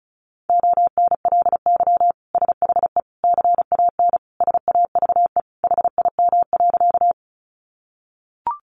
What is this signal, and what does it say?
Morse code, heard as a keyed tone, about 34 wpm, 700 Hz. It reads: ONLY SHE CAN SAVE HIM.
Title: Sentences from Top 1100 Words 35wpm